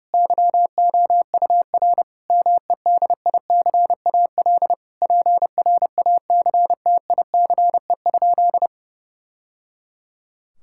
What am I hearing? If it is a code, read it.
Morse code, heard as a keyed tone, about 30 wpm, 700 Hz. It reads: YOUR MEDICAL PRACTICE?
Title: Sentences from Top 1000 Words - Review Part 5 - 30wpm